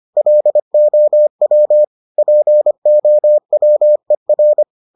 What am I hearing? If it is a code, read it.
Morse code, heard as a keyed tone, about 25 wpm, 600 Hz. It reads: LOW POWER